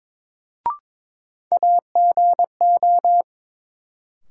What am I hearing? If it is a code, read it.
Morse code, heard as a keyed tone, about 22 wpm, 700 Hz. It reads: AGO
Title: Sets of 2 Words - New Word Drill - 22wpm